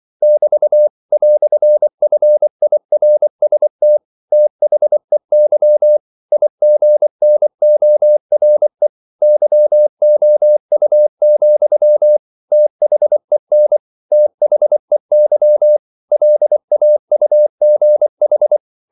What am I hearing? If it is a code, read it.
Morse code, heard as a keyed tone, about 24 wpm, 600 Hz. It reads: <BT> "FIRST THEY IGNORE YOU, THEN THEY LAUGH
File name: CW / Practice / EnglishQuotations0224